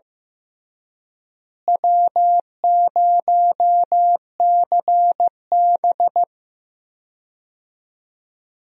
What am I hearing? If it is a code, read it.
Morse code, heard as a keyed tone, about 15 wpm, 700 Hz. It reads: W0CB
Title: USA Call Signs - 4 Characters - Repeated 3x - 15wpm